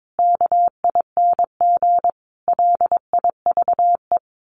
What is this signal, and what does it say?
Morse code, heard as a keyed tone, about 22 wpm, 700 Hz. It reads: KING LIVE
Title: Sets of 5 Words Encore2 22wpm